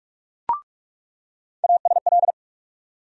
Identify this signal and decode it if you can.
Morse code, heard as a keyed tone, about 48 wpm, 700 Hz. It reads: ASL
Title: USA Call Sign Postfixes - 3 Letters 45wpm